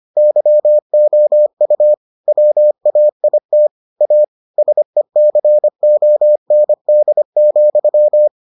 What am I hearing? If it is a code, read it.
Morse code, heard as a keyed tone, about 25 wpm, 600 Hz. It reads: YOU WAIT A SECOND,